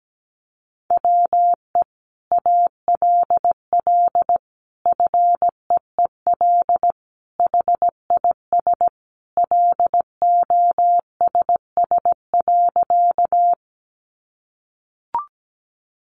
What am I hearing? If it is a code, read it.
Morse code, heard as a keyed tone, about 17 wpm, 700 Hz. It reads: WE ALL FEEL HIS LOSS.
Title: Sentences from Top 1400 Words 17wpm